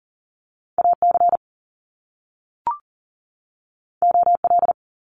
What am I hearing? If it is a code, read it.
Morse code, heard as a keyed tone, about 42 wpm, 700 Hz. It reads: AC OL
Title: Top Two Letter Combinations in Words 40wpm